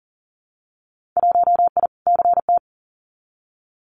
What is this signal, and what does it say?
Morse code, heard as a keyed tone, about 38 wpm, 700 Hz. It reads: 1I CT
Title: Winter Field Day Exchanges 40wpm